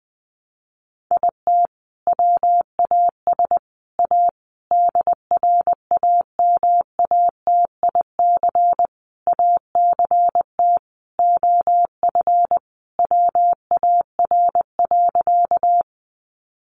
Morse code, heard as a keyed tone, about 20 wpm, 700 Hz. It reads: IT WAS A DRAMATIC ACT OF WAR.
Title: Sentences from Top 1300 Words 20wpm